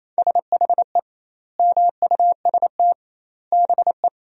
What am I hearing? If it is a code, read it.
Morse code, heard as a keyed tone, about 28 wpm, 700 Hz. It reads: SHE MUST BE
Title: 2x Word-Spacing - Sentences from Top 400 Words - 28wpm